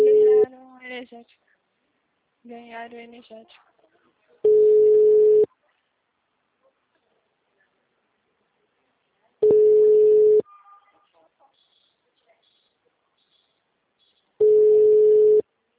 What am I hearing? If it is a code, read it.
Ringback tone (Europe, 425 Hz), with speech.